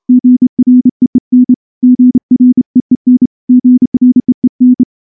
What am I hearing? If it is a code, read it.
Morse code, heard as a keyed tone, about 30 wpm, 250 Hz. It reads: GREEN GREEN GREEN
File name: morse-training-30WPM